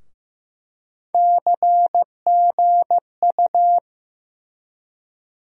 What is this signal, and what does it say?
Morse code, heard as a keyed tone, about 15 wpm, 700 Hz. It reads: CGU
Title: USA Call Sign Postfixes - 3 Letters 15wpm